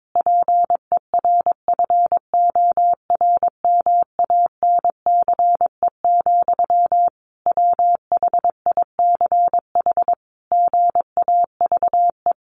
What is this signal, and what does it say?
Morse code, heard as a keyed tone, about 22 wpm, 700 Hz. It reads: PERFORMANCE, WHICH GAVE
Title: Sentences from Top 1500 Words 22wpm